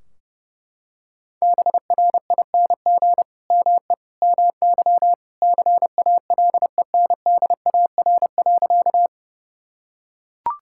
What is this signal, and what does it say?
Morse code, heard as a keyed tone, about 32 wpm, 700 Hz. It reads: BRING ME MY CALENDAR.
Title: Sentences from Top 1600 Words 30wpm